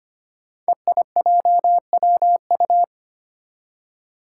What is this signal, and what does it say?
Morse code, heard as a keyed tone, about 25 wpm, 700 Hz. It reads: EIJWU
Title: Five Challenging Letter-Number Combinations 25wpm